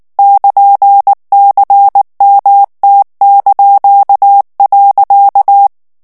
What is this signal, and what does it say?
Morse code, heard as a keyed tone, about 19 wpm, 800 Hz. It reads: <KN>CMT).
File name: morse mp3 / ESPAGNOL TEXTE EN CLAIR 2 VITESSE 18 MOTS MINUTES